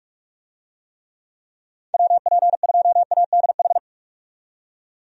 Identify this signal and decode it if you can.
Morse code, heard as a keyed tone, about 43 wpm, 700 Hz. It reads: WP2ADH
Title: USA Call Signs - 6 Characters - 45wpm